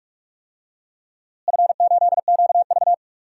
Morse code, heard as a keyed tone, about 46 wpm, 700 Hz. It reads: F8QV
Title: DXCC-100 Call Signs - 4 Characters - Repeated 3x - 45wpm